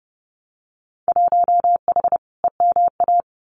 Morse code, heard as a keyed tone, about 30 wpm, 700 Hz. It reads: 1H EMA
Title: Winter Field Day Exchanges 30wpm